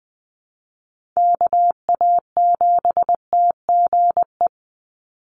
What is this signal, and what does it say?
Morse code, heard as a keyed tone, about 20 wpm, 700 Hz. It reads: KA7TGE
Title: USA Call Signs 20wpm